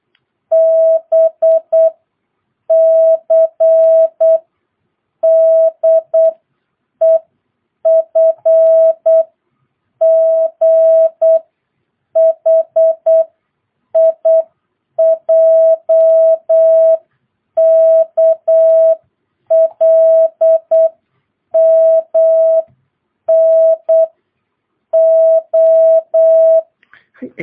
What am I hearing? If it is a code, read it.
Morse code, heard as a keyed tone, about 8 wpm, spaced as for 6 wpm, 650 Hz. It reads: BCDEFGHIJKLMNO
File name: jh8cbh / 2022cw